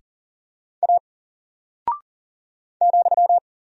Morse code, heard as a keyed tone, about 40 wpm, 700 Hz. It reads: A ,
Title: Single Character (Letter:Number:Punctuation:Prosign) Alt 40wpm